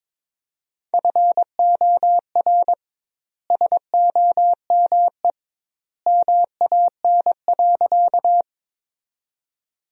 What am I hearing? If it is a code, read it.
Morse code, heard as a keyed tone, about 21 wpm, 700 Hz. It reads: FOR SOME MAN.
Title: 2x Word-Spacing - Sentences from Top 200 Words - 22wpm